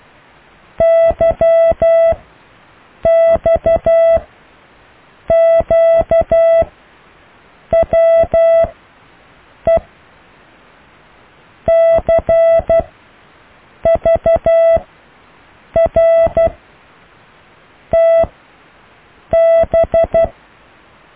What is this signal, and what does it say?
Morse code, heard as a keyed tone, about 12 wpm, spaced as for 6 wpm, 660 Hz. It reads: YXQWE CVRTB
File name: cw / Alle letters